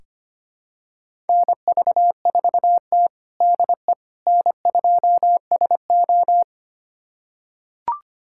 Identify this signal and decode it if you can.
Morse code, heard as a keyed tone, about 25 wpm, 700 Hz. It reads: NV4T DE N2SO
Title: USA Callsign DE Callsign - 4 Characters - 25wpm